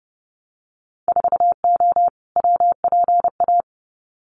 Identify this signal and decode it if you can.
Morse code, heard as a keyed tone, about 30 wpm, 700 Hz. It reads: 4O WPA
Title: Winter Field Day Exchanges 30wpm